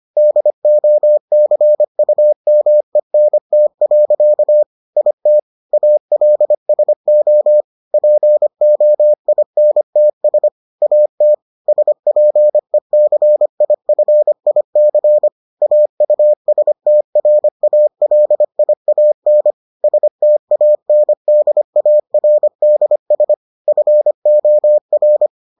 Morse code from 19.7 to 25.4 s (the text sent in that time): STANDARDS FOR